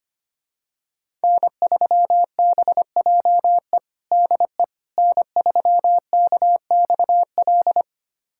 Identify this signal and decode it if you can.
Morse code, heard as a keyed tone, about 25 wpm, 700 Hz. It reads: N3BJE DE N3KXL
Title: USA Callsign DE Callsign - 5 Characters - 25wpm